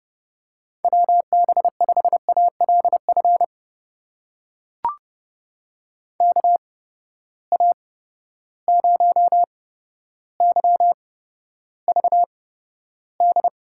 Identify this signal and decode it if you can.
Morse code, heard as a keyed tone, about 30 wpm, spaced as for 13 wpm, 700 Hz. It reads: WB5ALF KA0YVD
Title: USA Call Signs - 6 Characters - 8x Character Spacing - 30wpm